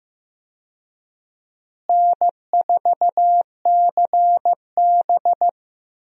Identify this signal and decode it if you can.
Morse code, heard as a keyed tone, about 15 wpm, 700 Hz. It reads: N4CB
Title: USA Call Signs - 4 Characters - Repeated 3x - 15wpm